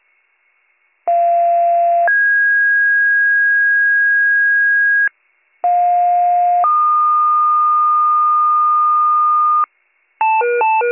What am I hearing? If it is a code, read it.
Two-tone page, heard as a one-second tone, then a three-second tone, about 690 then 1650 Hz, about 710 then 1220 Hz, with noise. It is heard from 1.1 to 9.6 s.